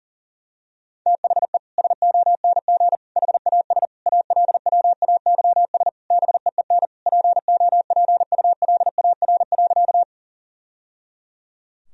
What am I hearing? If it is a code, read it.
Morse code, heard as a keyed tone, about 40 wpm, 700 Hz. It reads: THE SONG HAS ALWAYS BEEN POPULAR.
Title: Sentences from Top 1000 Words - Review Part 3 - 40wpm